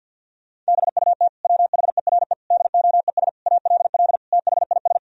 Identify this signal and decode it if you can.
Morse code, heard as a keyed tone, about 52 wpm, 700 Hz. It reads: BUT WHERE DOES ALL THIS
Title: Sentences from Top 1000 Words - Review Part 2 - 50wpm